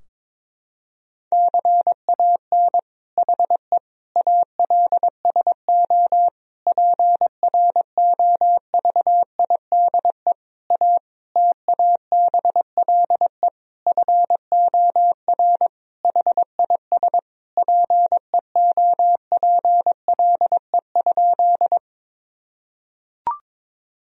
Morse code, heard as a keyed tone, about 22 wpm, 700 Hz. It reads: CAN HE ALSO PROVIDE A TABLE FOR HIS PEOPLE?
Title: Sentences from Top 1000 Words - Review Part 3 - 22wpm